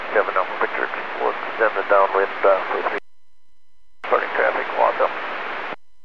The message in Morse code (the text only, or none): none